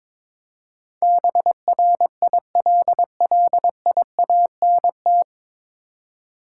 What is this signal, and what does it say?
Morse code, heard as a keyed tone, about 22 wpm, 700 Hz. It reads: BRILLIANT
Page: Morse Code Ninja practice 22wpm